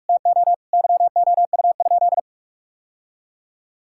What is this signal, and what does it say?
Morse code, heard as a keyed tone, about 44 wpm, 700 Hz. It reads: TO YOU?
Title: Sentences from Top 1000 Words - Review Part 6 - 45wpm